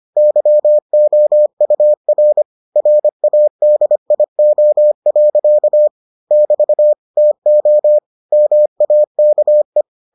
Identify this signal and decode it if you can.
Morse code, heard as a keyed tone, about 25 wpm, 600 Hz. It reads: YOUR RADIO. <BT> TO MAKE